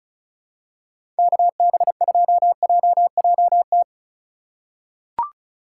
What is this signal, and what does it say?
Morse code, heard as a keyed tone, about 36 wpm, 700 Hz. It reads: KB2JJT
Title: USA Call Signs - 6 Characters - 3x Character Spacing - 35wpm